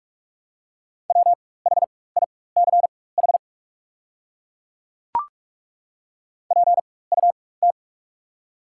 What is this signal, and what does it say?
Morse code, heard as a keyed tone, about 42 wpm, spaced as for 21 wpm, 690 Hz. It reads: WHICH PUT
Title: 4x Character Spacing - Top 300 Words - 45wpm